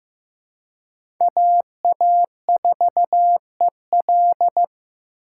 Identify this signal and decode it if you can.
Morse code, heard as a keyed tone, about 15 wpm, 700 Hz. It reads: AA4EL
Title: USA Call Signs - 5 Characters - 4x Character Spacing - 15wpm